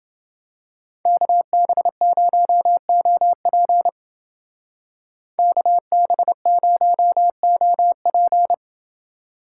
Morse code, heard as a keyed tone, about 28 wpm, 700 Hz. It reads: KB0OP KB0OP